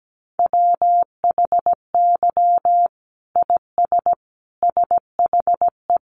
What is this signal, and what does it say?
Morse code, heard as a keyed tone, about 17 wpm, 700 Hz. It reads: WHY IS SHE
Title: Sentences from Top 900 Words 17wpm